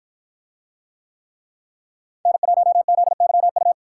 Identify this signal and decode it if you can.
Morse code, heard as a keyed tone, about 50 wpm, 700 Hz. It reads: N1ZYU